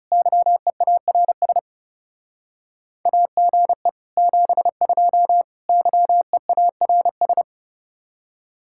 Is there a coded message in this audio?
Yes: Morse code from 0.1 to 7.4 s, heard as a keyed tone, about 38 wpm, 700 Hz. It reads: YEARS AGE 72 YEARS